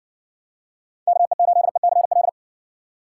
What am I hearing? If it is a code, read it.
Morse code, heard as a keyed tone, about 60 wpm, 700 Hz. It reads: KE8ECL